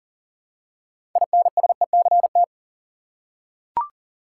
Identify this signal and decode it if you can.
Morse code, heard as a keyed tone, about 40 wpm, 700 Hz. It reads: INSECT
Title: Six Letter Words - 40wpm